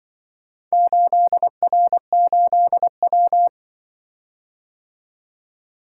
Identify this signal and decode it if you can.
Morse code, heard as a keyed tone, about 24 wpm, 700 Hz. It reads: 8R8W
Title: Speed-Racing - Four Letter-Number Combinations - 20wpm